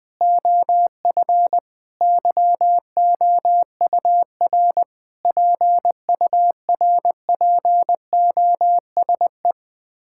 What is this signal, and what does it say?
Morse code, heard as a keyed tone, about 19 wpm, 700 Hz. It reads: OF YOUR PURPOSE
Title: Sets of 9 Words - 20wpm